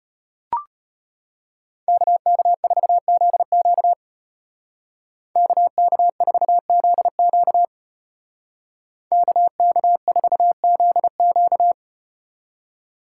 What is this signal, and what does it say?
Morse code, heard as a keyed tone, about 33 wpm, 700 Hz. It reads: KK4ZQ KK4ZQ KK4ZQ